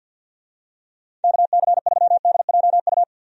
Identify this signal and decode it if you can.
Morse code, heard as a keyed tone, about 52 wpm, 700 Hz. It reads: KC3DJU
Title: USA Call Signs - 6 Characters - Repeated 3x - 50wpm